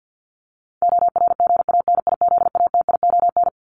Morse code, heard as a keyed tone, about 50 wpm, 700 Hz. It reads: ORGANIZATION
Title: Top 501-750 Words 50wpm